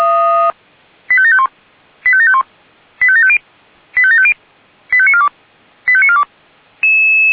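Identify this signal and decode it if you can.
Five-tone selective call (ZVEI), heard as quick runs of five tones, tones of about 70 ms. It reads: 85751 85751 85750 85750 85942 85942
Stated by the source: ZVEI five-tone call 85730 received by SDR.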